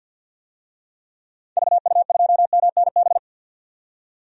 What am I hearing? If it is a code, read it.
Morse code, heard as a keyed tone, about 52 wpm, 700 Hz. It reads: VU2MNB
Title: DXCC-100 Call Signs - 6 Characters - 50wpm